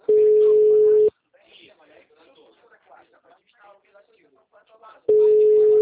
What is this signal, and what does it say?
Ringback tone (Europe, 425 Hz), with speech.